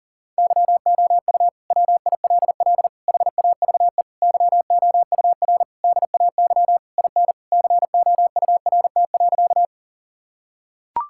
Morse code, heard as a keyed tone, about 40 wpm, 700 Hz. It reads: YOU WILL HAVE YOUR DAY IN COURT.
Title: Sentences from Top 1600 Words 40wpm